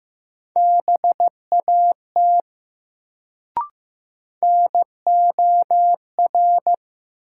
Morse code, heard as a keyed tone, about 15 wpm, 700 Hz. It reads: BAT NOR
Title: Sentences from Top 1500 Words - New Word Drill Alt - 15wpm